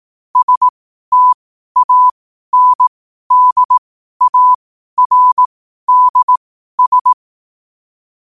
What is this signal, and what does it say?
Morse code, heard as a keyed tone, about 18 wpm, spaced as for 12 wpm, 1000 Hz. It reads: STANDARDS
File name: qotdfiles / 12wpm